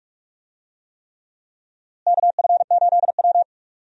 Morse code, heard as a keyed tone, about 41 wpm, 700 Hz. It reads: KF8W
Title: USA Call Signs - 4 Characters - Repeated 3x - 45wpm